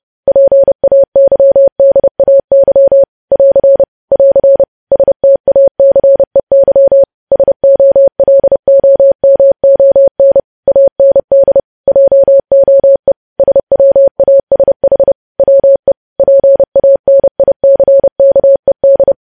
Morse code, heard as a keyed tone, about 30 wpm, 550 Hz. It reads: PAYDAY <AR> <AR> STACEY SOLOMON AND JOE SWASH WE PANICKED
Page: news headlines in morse code at 30 wpm